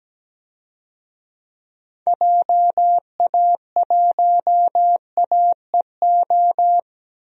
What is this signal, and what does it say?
Morse code, heard as a keyed tone, about 17 wpm, 700 Hz. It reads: JA1AEO
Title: DXCC-100 Call Signs - 6 Characters - Repeated 2x - 17wpm